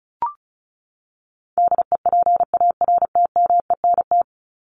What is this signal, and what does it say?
Morse code, heard as a keyed tone, about 34 wpm, 700 Hz. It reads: DEPARTMENT
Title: Top 751-1000 Words 35wpm